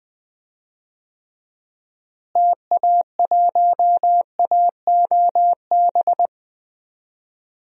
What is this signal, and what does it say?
Morse code, heard as a keyed tone, about 21 wpm, 700 Hz. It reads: TA1AOB
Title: DXCC-100 Call Signs - 6 Characters - Repeated 2x - 20wpm